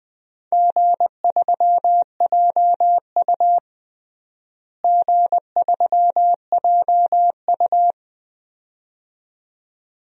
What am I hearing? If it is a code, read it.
Morse code, heard as a keyed tone, about 21 wpm, 700 Hz. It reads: G3JU G3JU
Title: DXCC-100 Call Signs - 4 Characters - Repeated 2x - 20wpm